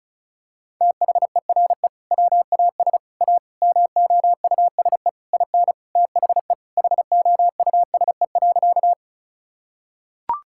Morse code, heard as a keyed tone, about 34 wpm, 700 Hz. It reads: THERE WAS A MOUSE IN THE HOUSE.
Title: Sentences from Top 1000 Words - Review Part 5 - 35wpm